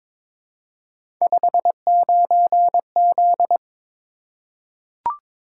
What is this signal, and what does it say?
Morse code, heard as a keyed tone, about 22 wpm, 700 Hz. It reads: 59Z
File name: practice / 22wpm / Three Letter-Number Combinations - 22wpm